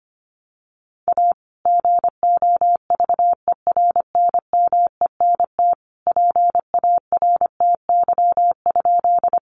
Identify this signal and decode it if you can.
Morse code, heard as a keyed tone, about 25 wpm, 700 Hz. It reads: A GOVERNMENT PARTY?
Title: Sentences from Top 1400 Words 25wpm